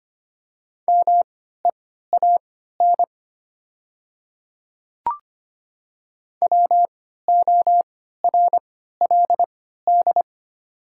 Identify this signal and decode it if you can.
Morse code, heard as a keyed tone, about 25 wpm, spaced as for 14 wpm, 700 Hz. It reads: MEAN WORLD
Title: 3x Character Spacing - Top 200 Words - 25wpm